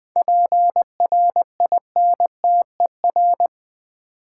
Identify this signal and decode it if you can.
Morse code, heard as a keyed tone, about 20 wpm, 700 Hz. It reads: PRINTER
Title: Easy Words with Hints - 20wpm